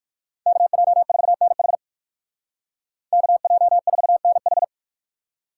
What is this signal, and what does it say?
Morse code, heard as a keyed tone, about 57 wpm, 700 Hz. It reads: KJ4NH KJ4NH